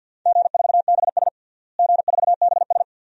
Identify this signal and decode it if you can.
Morse code, heard as a keyed tone, about 52 wpm, 700 Hz. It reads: G4BS G4BS
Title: DXCC-100 Call Signs - 4 Characters - Repeated 2x - 50wpm